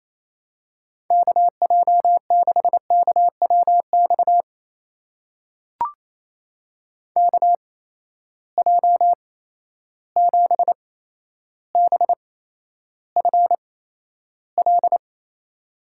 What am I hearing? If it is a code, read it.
Morse code, heard as a keyed tone, about 28 wpm, spaced as for 12 wpm, 700 Hz. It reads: KJ6KWX KJ7BFL
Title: USA Call Signs - 6 Characters - 8x Character Spacing - 28wpm